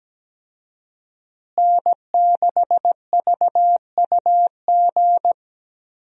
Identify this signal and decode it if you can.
Morse code, heard as a keyed tone, about 17 wpm, 700 Hz. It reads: N6VUG
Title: USA Call Signs - 5 Characters - Repeated 2x - 17wpm